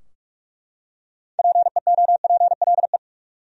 Morse code, heard as a keyed tone, about 43 wpm, 700 Hz. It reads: PEOPLE
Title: Top 100 Words 45wpm